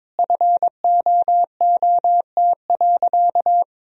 Morse code, heard as a keyed tone, about 22 wpm, 700 Hz. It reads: FOOT.
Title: Sentences from Top 1000 Words - Review Part 3 - 22wpm